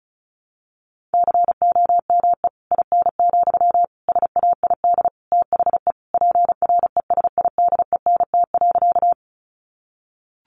Morse code, heard as a keyed tone, about 34 wpm, 700 Hz. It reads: COME IN, SAID THE PRESIDENT.
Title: Sentences from Top 1000 Words 35wpm